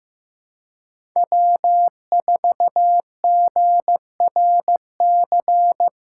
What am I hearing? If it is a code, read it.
Morse code, heard as a keyed tone, about 15 wpm, 700 Hz. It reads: W4GRC
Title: USA Callsign DE Callsign - 5 Characters - 15wpm